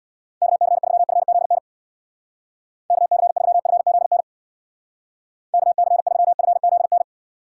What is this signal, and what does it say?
Morse code, heard as a keyed tone, about 60 wpm, 700 Hz. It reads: KC3FZN KC3FZN KC3FZN
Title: Speed Racing - USA 2x3 Call Signs 50wpm